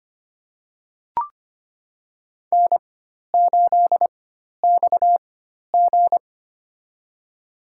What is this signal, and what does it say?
Morse code, heard as a keyed tone, about 25 wpm, spaced as for 12 wpm, 700 Hz. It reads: N8XG
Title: USA Call Signs - 4 Characters - 4x Character Spacing - 25wpm